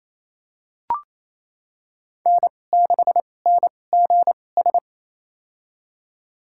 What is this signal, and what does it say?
Morse code, heard as a keyed tone, about 28 wpm, spaced as for 20 wpm, 700 Hz. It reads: N6NGS
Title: USA Call Signs - 5 Characters - 2x Character Spacing - 28wpm